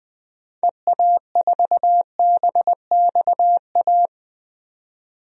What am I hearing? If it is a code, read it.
Morse code, heard as a keyed tone, about 20 wpm, 700 Hz. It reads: EA4BXA
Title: DXCC-100 Call Signs - 6 Characters - 20wpm